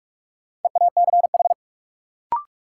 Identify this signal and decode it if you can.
Morse code, heard as a keyed tone, about 46 wpm, 700 Hz. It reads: EACH
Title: Sentences from Top 200 Words - New Word Drill - 45wpm